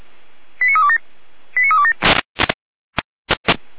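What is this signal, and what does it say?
Five-tone selective call (ZVEI), heard as quick runs of five tones, tones of about 70 ms. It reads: 89327 89327
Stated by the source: SDR-received ZVEI five-tone call 89333.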